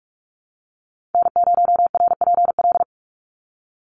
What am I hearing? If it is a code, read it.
Morse code, heard as a keyed tone, about 44 wpm, 700 Hz. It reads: N0RPL
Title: USA Call Signs 45wpm